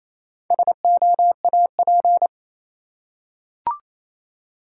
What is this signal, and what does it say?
Morse code, heard as a keyed tone, about 27 wpm, 700 Hz. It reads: SOAP
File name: Easy Words - Speed-Racing with Hints - 22wpm